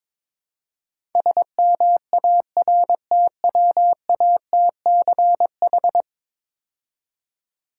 Morse code, heard as a keyed tone, about 22 wpm, 700 Hz. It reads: SMARTWATCH
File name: Easy Words - Speed-Racing with Hints - 22wpm